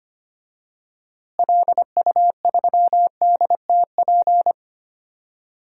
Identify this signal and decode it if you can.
Morse code, heard as a keyed tone, about 25 wpm, 700 Hz. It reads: LU3DTP
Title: DXCC-100 Call Signs - 6 Characters - Repeated 3x - 25wpm